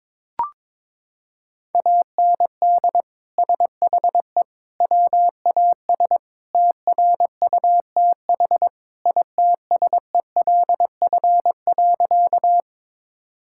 Morse code, heard as a keyed tone, about 22 wpm, 700 Hz. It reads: AND SHE WAS TRUTH ITSELF.
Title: Sentences from Top 1000 Words - Review Part 5 - 22wpm